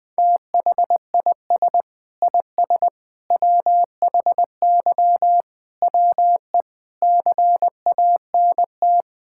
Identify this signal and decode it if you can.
Morse code, heard as a keyed tone, about 21 wpm, 700 Hz. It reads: THIS IS WHY WE CANT
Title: Sentences with 8-Word Phrases - 20wpm